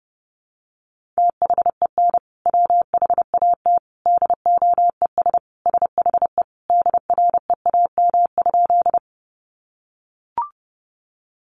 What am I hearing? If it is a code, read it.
Morse code, heard as a keyed tone, about 30 wpm, 700 Hz. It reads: THEN WHAT DOES SHE DREAM?
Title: Sentences from Top 1500 Words 30wpm